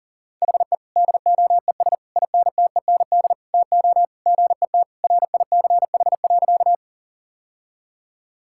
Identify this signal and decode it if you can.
Morse code, heard as a keyed tone, about 40 wpm, 700 Hz. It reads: HE DOES INTEND TO GET RICH.
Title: Sentences from Top 1600 Words 40wpm